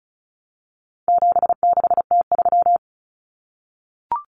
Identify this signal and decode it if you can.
Morse code, heard as a keyed tone, about 34 wpm, 700 Hz. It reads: 76T3
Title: Common Cut Numbers - 4-Digit 35wpm